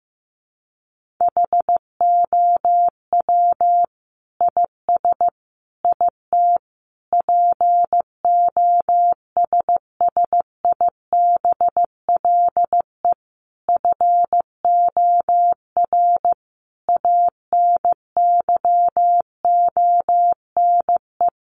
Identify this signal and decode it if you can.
Morse code, heard as a keyed tone, about 15 wpm, 700 Hz. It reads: HOW IS IT POSSIBLE FOR ANYONE
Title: Sentences from Top 900 Words 15wpm